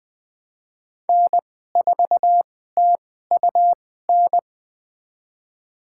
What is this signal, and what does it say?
Morse code, heard as a keyed tone, about 21 wpm, spaced as for 15 wpm, 700 Hz. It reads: N4TUN
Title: USA Call Signs - 5 Characters - 2x Character Spacing - 20wpm